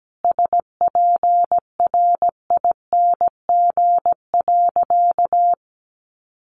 Morse code, heard as a keyed tone, about 17 wpm, 700 Hz. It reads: SPRING.
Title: Sentences from Top 1200 Words 17wpm